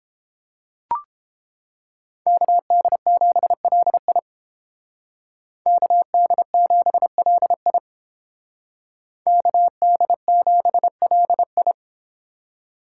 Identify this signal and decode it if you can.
Morse code, heard as a keyed tone, about 31 wpm, 700 Hz. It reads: KD7LI KD7LI KD7LI